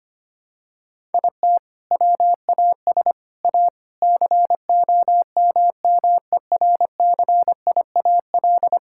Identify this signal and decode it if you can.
Morse code, heard as a keyed tone, about 25 wpm, 700 Hz. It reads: IT WAS A COMMERCIAL
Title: Sentences from Top 1000 Words - Review Part 5 - 25wpm